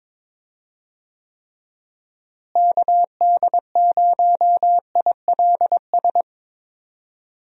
Morse code, heard as a keyed tone, about 22 wpm, 700 Hz. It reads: KD0ILS
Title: USA Call Signs - 6 Characters - Repeated 3x - 22wpm